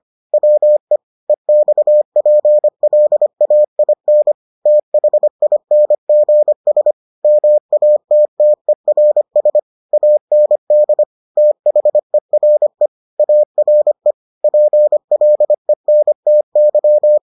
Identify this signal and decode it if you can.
Morse code, heard as a keyed tone, about 25 wpm, 600 Hz. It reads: WE EXPLAIN THINGS MATTERS AND THERE ARE PLENTY